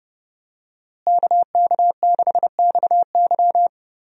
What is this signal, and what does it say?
Morse code, heard as a keyed tone, about 32 wpm, 700 Hz. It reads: KK6XY
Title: USA Call Signs - 5 Characters - Repeated 2x - 30wpm